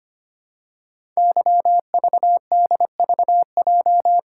Morse code, heard as a keyed tone, about 25 wpm, 700 Hz. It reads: YVDVJ
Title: Code Groups - 5 Letters - 25wpm